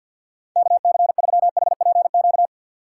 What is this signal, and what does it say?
Morse code, heard as a keyed tone, about 48 wpm, 700 Hz. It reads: KC3HPQ
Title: USA Call Signs - 6 Characters - 3x Character Spacing - 50wpm